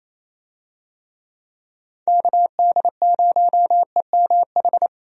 Morse code, heard as a keyed tone, about 28 wpm, 700 Hz. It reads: KD0EMH
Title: USA Call Signs - 6 Characters - 28wpm